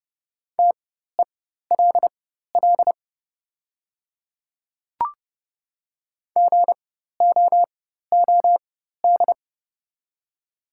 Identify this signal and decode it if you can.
Morse code, heard as a keyed tone, about 30 wpm, spaced as for 14 wpm, 700 Hz. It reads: TELL GOOD
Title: 4x Character Spacing - Top 200 Words - 30wpm